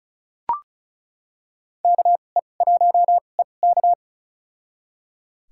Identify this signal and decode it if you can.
Morse code, heard as a keyed tone, about 35 wpm, spaced as for 25 wpm, 700 Hz. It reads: KE1EK
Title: USA Call Signs - 5 Characters - 2x Character Spacing - 35wpm